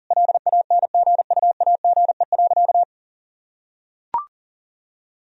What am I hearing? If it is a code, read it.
Morse code, heard as a keyed tone, about 37 wpm, 700 Hz. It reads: LANGUAGE.
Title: Sentences from Top 1000 Words - Review Part 3 - 40wpm